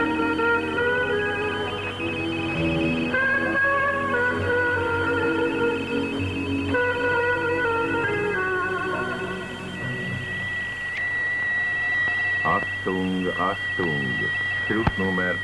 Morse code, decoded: none